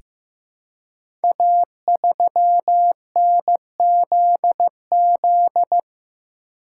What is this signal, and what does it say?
Morse code, heard as a keyed tone, about 15 wpm, 700 Hz. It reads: A3NZZ
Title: Code Groups - 5 Letter:Numbers - 15wpm